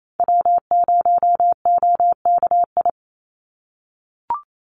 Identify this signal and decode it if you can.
Morse code, heard as a keyed tone, about 28 wpm, 700 Hz. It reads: W0OKI
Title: USA Call Signs 28wpm